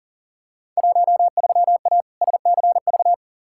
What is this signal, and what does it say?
Morse code, heard as a keyed tone, about 40 wpm, 700 Hz. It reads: 13A SCV
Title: Field Day Exchanges Encore - 40wpm